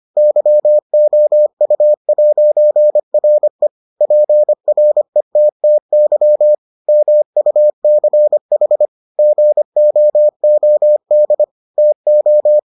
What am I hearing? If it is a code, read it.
Morse code, heard as a keyed tone, about 25 wpm, 600 Hz. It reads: YOU'RE PRETTY MUCH GOOD TO